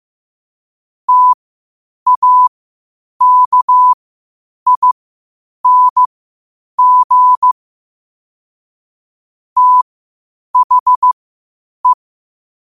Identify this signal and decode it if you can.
Morse code, heard as a keyed tone, about 15 wpm, spaced as for 8 wpm, 1000 Hz. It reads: TAKING THE
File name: qotdfiles / 8wpm